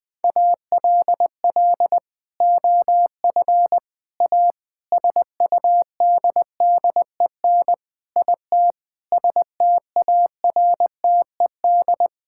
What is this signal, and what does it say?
Morse code, heard as a keyed tone, about 20 wpm, 700 Hz. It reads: ALL OF A SUDDEN IT STARTED
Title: Sentences with 4-Word Phrases - 20wpm